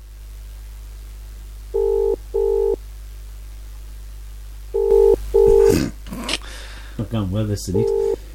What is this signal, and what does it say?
Ringback tone (UK), 400 + 450 Hz, with speech.